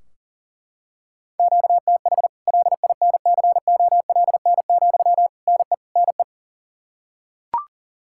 Morse code, heard as a keyed tone, about 40 wpm, 700 Hz. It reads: QTH LINCOLN, NE NE
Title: Practice QSO - Exchange 1 - Beginning - 40wpm